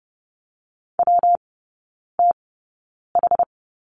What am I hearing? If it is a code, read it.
Morse code, heard as a keyed tone, about 30 wpm, spaced as for 9 wpm, 700 Hz. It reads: WTH
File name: Lesson 19e - Letter w - Three Characters 30wpm